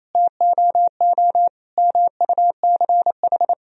Morse code, heard as a keyed tone, about 27 wpm, 700 Hz. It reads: TOO MUCH
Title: Sentences from Top 1000 Words - Review Part 1 - 28wpm